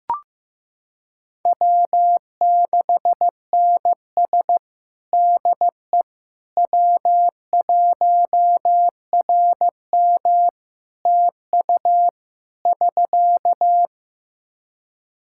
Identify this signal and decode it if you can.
Morse code, heard as a keyed tone, about 15 wpm, 700 Hz. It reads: W6NS DE W1RM TU <SK>
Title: K1USN SST QSO Practice 15wpm